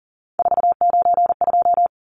Morse code, heard as a keyed tone, about 40 wpm, 700 Hz. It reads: 492
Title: Numbers - 3-Digit 40wpm